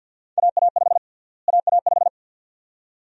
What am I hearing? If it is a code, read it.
Morse code, heard as a keyed tone, about 50 wpm, 700 Hz. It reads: AA5 AA5
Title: USA Call Sign Prefixes - Repeated 2x - 50wpm